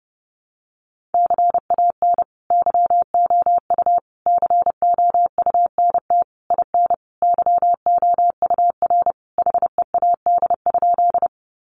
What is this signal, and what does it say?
Morse code, heard as a keyed tone, about 30 wpm, 700 Hz. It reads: CAN YOU COUNT IN YOUR HEAD?
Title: Sentences from Top 1500 Words 30wpm